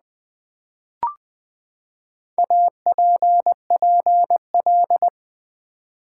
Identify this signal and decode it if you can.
Morse code, heard as a keyed tone, about 20 wpm, 700 Hz. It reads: APPL
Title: Build A Word - 4 Letters At A Time - 11-Letter Words - 20wpm